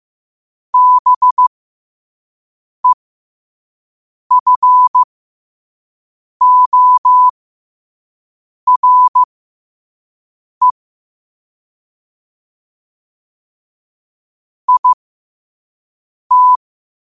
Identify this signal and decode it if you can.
Morse code, heard as a keyed tone, about 15 wpm, spaced as for 5 wpm, 1000 Hz. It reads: BEFORE IT
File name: qotdfiles / 5wpm